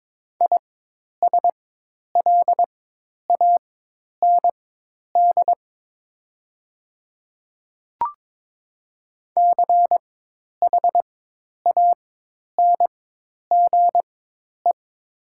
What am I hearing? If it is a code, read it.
Morse code, heard as a keyed tone, about 22 wpm, spaced as for 10 wpm, 700 Hz. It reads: ISLAND CHANGE
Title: 4x Character Spacing - Top 500 Words - 22wpm